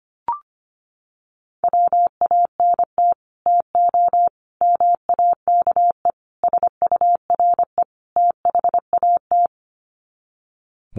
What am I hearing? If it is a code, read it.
Morse code, heard as a keyed tone, about 25 wpm, 700 Hz. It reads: WANT TO MAKE SURE THAT
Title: Sets of 5 Words 25wpm